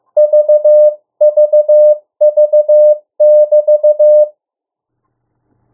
Morse code, heard as a keyed tone, about 15 wpm, 600 Hz. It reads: VVV<BT>
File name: morse mp3 / 02 urok 1